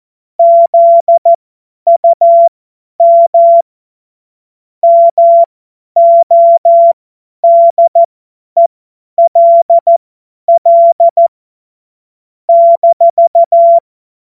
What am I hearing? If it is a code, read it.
Morse code, heard as a keyed tone, about 14 wpm, spaced as for 10 wpm, 680 Hz. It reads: ZUM MODELL -
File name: Morsekurs / Witze / Witz 19 Dezember 2023 50 F70